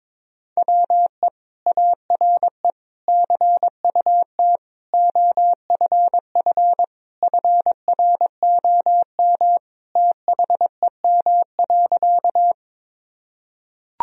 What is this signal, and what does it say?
Morse code, heard as a keyed tone, about 21 wpm, 700 Hz. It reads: WE ARE CUT OFF FROM THEM.
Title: Sentences from Top 1000 Words - Review Part 1 - 22wpm